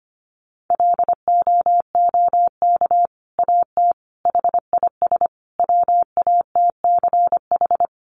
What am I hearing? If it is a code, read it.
Morse code, heard as a keyed tone, about 25 wpm, 700 Hz. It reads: LOOK AT HIS WATCH
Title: Sentences from Top 1500 Words 25wpm